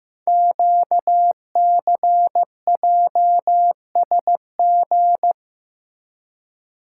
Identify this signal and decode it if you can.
Morse code, heard as a keyed tone, about 15 wpm, 700 Hz. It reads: QCJSG
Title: Code Groups - 5 Letters - 15wpm